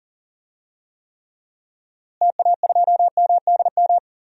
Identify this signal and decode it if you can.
Morse code, heard as a keyed tone, about 40 wpm, 700 Hz. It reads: TA2MDM
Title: DXCC-100 Call Signs - 6 Characters - Repeated 2x - 40wpm